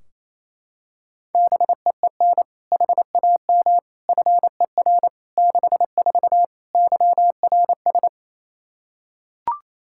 Morse code, heard as a keyed tone, about 28 wpm, 700 Hz. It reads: BEEN HAM FER 64 YRS
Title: K1USN SST QSO Practice 28wpm